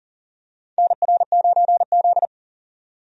Morse code, heard as a keyed tone, about 40 wpm, 700 Hz. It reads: NR9Z
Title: USA Call Signs - 4 Characters - 40wpm